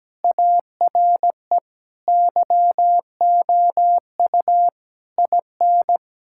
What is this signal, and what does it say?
Morse code, heard as a keyed tone, about 17 wpm, 700 Hz. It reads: ARE YOU IN
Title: Sentences from Top 1000 Words - Review Part 6 - 17wpm